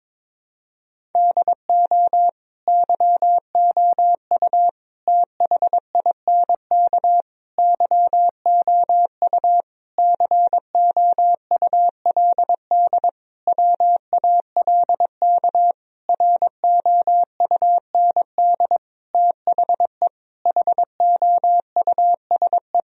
Morse code, heard as a keyed tone, about 22 wpm, 700 Hz. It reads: DO YOU THINK YOU COULD WALK ROUND THE HOUSE
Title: Sentences from Top 1000 Words - Review Part 1 - 22wpm